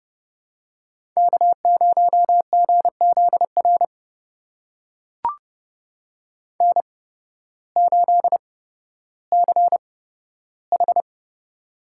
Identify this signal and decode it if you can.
Morse code, heard as a keyed tone, about 30 wpm, spaced as for 14 wpm, 700 Hz. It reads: K0GZR N8CH
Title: USA Call Signs - 8x Character Spacing - 30wpm